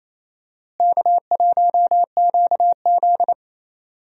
Morse code, heard as a keyed tone, about 27 wpm, 700 Hz. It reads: K1QZ
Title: USA Call Signs - 4 Characters - Repeated 3x - 28wpm